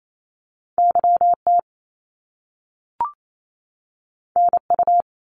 Morse code, heard as a keyed tone, about 28 wpm, 700 Hz. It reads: YT NU
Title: Canadian Provinces and Territories 28wpm